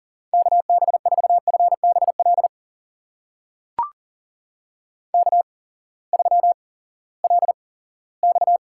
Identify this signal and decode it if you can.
Morse code, heard as a keyed tone, about 40 wpm, spaced as for 20 wpm, 700 Hz. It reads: KB4FBL K3LX
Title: USA Call Signs - 8x Character Spacing - 40wpm